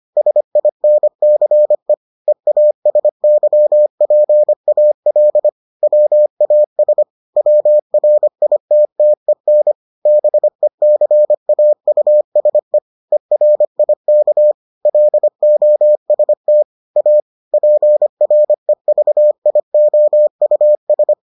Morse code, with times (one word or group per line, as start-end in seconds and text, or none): 0.2-1.9 SINCE
2.3-5.5 EASYPAL
5.8-7.0 WAS
7.4-9.7 WRITTEN
10.1-12.8 BECAUSE
13.1-14.5 ERIK
14.9-16.6 LOST
17.0-17.2 A
17.5-21.1 PREVIOUS